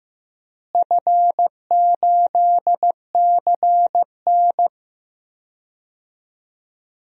Morse code, heard as a keyed tone, about 16 wpm, 700 Hz. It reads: F8CN
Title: DXCC-100 Call Signs - 4 Characters - Repeated 3x - 15wpm